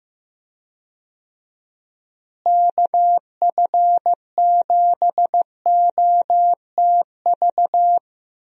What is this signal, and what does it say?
Morse code, heard as a keyed tone, about 15 wpm, 700 Hz. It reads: KF7OTV
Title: USA Call Signs - 6 Characters - Repeated 2x - 15wpm